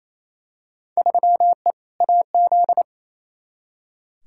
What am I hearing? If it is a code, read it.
Morse code, heard as a keyed tone, about 28 wpm, 700 Hz. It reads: 3E AZ
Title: Field Day Exchanges Encore - 28wpm